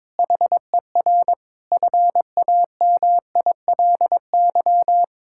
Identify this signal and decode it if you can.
Morse code, heard as a keyed tone, about 22 wpm, 700 Hz. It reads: HER FAMILY
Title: Sentences from Top 1600 Words 22wpm